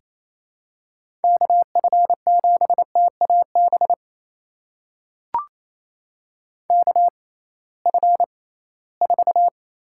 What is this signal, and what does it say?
Morse code, heard as a keyed tone, about 28 wpm, spaced as for 18 wpm, 700 Hz. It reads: KF7TAB KF4
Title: USA Call Signs - 6 Characters - 6x Character Spacing - 28wpm